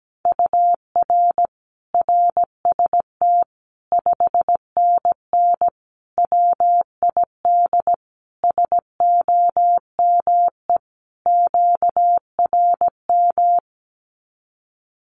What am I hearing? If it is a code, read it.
Morse code, heard as a keyed tone, about 17 wpm, 700 Hz. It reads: UR RST 5NN WID SOME QRM
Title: Common QSO Phrases 17wpm